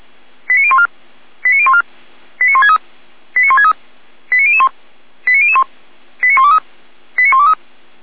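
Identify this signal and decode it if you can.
Five-tone selective call (ZVEI), heard as quick runs of five tones, tones of about 70 ms. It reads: 89014 89014 89163 89163 89001 89001 89123 89123
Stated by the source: ZVEI five-tone selective call received by SDR.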